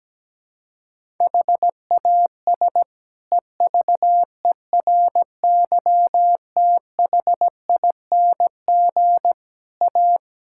Morse code, heard as a keyed tone, about 17 wpm, 700 Hz. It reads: HAS EVERYTHING A
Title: Sentences from Top 1600 Words 17wpm